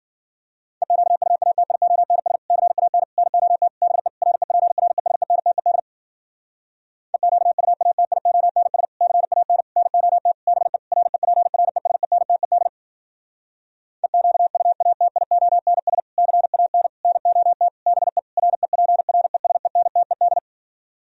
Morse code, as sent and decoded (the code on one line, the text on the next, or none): . --.- ..- .- - .. --- -. ... / -.-. .- -. / -. --- - / -... / .-. .--. .-. ... -. - . -.. / . --.- ..- .- - .. --- -. ... / -.-. .- -. / -. --- - / -... / .-. .--. .-. ... -. - . -.. / . --.- ..- .- - .. --- -. ... / -.-. .- -. / -. --- - / -... / .-. .--. .-. ... -. - . -..
EQUATIONS CAN NOT BE REPRESENTED EQUATIONS CAN NOT BE REPRESENTED EQUATIONS CAN NOT BE REPRESENTED